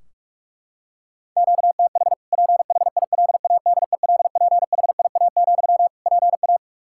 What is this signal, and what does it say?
Morse code, heard as a keyed tone, about 41 wpm, 700 Hz. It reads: QTH PHILADELPHIA, PA
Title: Practice QSO - Exchange 1 - Beginning - 45wpm